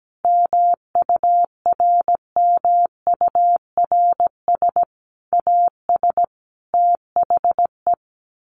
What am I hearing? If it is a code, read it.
Morse code, heard as a keyed tone, about 17 wpm, 700 Hz. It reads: MURMURS AS THE
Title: A Sound Of Thunder 17wpm